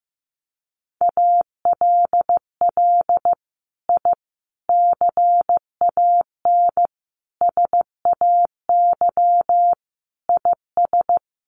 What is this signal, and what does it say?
Morse code, heard as a keyed tone, about 15 wpm, 700 Hz. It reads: ALL I CAN SAY IS
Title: Sets of 5 Words Encore2 15wpm